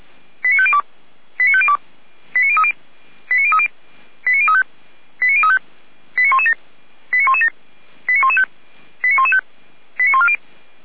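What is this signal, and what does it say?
Five-tone selective call (ZVEI), heard as quick runs of five tones, tones of about 70 ms. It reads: 89502 89502 89030 89030 89035 89035 89117 89117 89115 89115 89140
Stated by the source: Received on SDR; ZVEI five-tone call.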